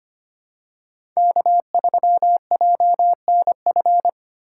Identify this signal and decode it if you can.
Morse code, heard as a keyed tone, about 25 wpm, 700 Hz. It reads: K3JNF
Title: USA Call Signs - 4x Character Spacing - 25wpm